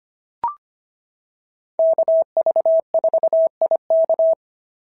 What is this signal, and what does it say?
Morse code, heard as a keyed tone, about 24 wpm, 650 Hz. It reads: KV4IK
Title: SOTA QSO Practice 25wpm